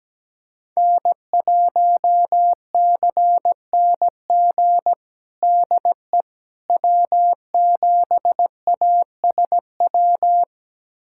Morse code, heard as a keyed tone, about 17 wpm, 700 Hz. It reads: N1CNG DE W7ASW
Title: USA Callsign DE Callsign - 5 Characters - 17wpm